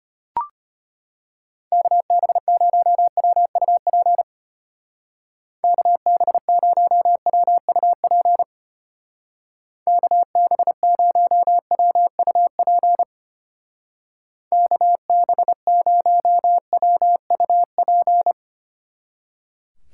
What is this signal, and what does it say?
Morse code, heard as a keyed tone, about 31 wpm, 700 Hz. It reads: KB0WUP KB0WUP KB0WUP KB0WUP
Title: Speed Racing - USA 2x3 Call Signs 25wpm